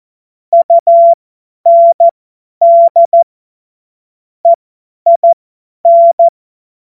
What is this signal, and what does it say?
Morse code, heard as a keyed tone, about 14 wpm, spaced as for 10 wpm, 680 Hz. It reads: UND EIN
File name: Morsekurs / Witze / Witz 21. Juni 2022 50 F70